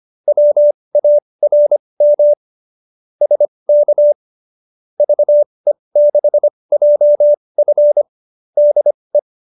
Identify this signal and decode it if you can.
Morse code, heard as a keyed tone, about 25 wpm, spaced as for 20 wpm, 590 Hz. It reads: WARM SK VE6JF DE